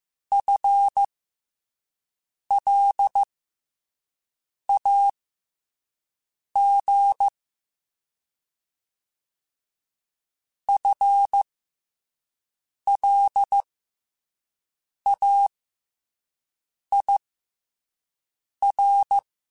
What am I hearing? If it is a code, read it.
Morse code, heard as a keyed tone, about 15 wpm, spaced as for 5 wpm, 800 Hz. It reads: FLAG FLAIR
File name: FL Runs / FL Runs mp3s / FL Runs @5